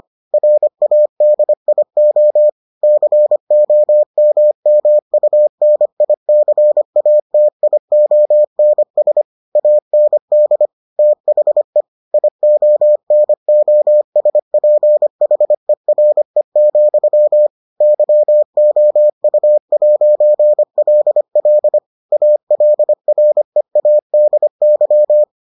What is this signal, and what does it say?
Morse code, heard as a keyed tone, about 25 wpm, 600 Hz. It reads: RADIO COMMUNICATIONS AND THE IONOSPHERE, YOU'LL ALREADY